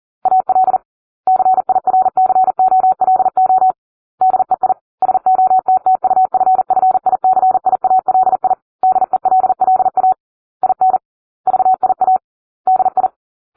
Morse code, heard as a keyed tone, about 40 wpm, 750 Hz. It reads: AL CIRCOLO DEI SOTTUFFICIALI DELLA IN VIA DI